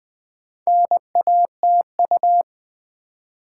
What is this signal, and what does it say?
Morse code, heard as a keyed tone, about 20 wpm, 700 Hz. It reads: NATU
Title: Build A Word - 2 Letters At A Time - 6-Letter Words - 20wpm